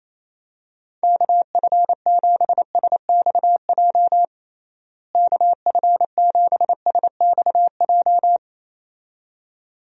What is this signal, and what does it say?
Morse code, heard as a keyed tone, about 28 wpm, 700 Hz. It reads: KF7SXJ KF7SXJ
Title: USA Call Signs - 6 Characters - Repeated 2x - 28wpm